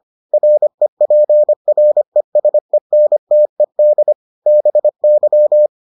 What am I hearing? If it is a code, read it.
Morse code, heard as a keyed tone, about 25 wpm, 600 Hz. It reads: REPRESENTED BY